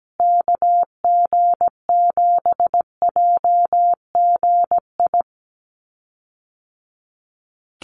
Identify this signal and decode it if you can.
Morse code, heard as a keyed tone, about 17 wpm, 700 Hz. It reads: KG7JGI
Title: USA Call Signs Repeated 3x 17wpm